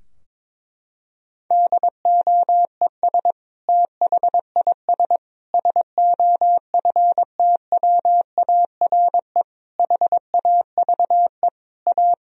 Morse code, heard as a keyed tone, about 22 wpm, 700 Hz. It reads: DOES THIS SOFTWARE HAVE A
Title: Sentences from Top 1000 Words - Review Part 3 - 22wpm